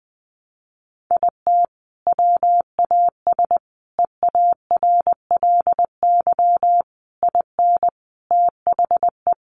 Morse code, heard as a keyed tone, about 20 wpm, 700 Hz. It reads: IT WAS EARLY IN THE
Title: Sentences from Top 1200 Words 20wpm